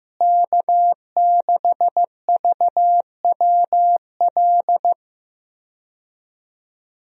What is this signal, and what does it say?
Morse code, heard as a keyed tone, about 15 wpm, 700 Hz. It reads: K6VWL
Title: USA Call Signs - 5 Characters - Repeated 2x - 15wpm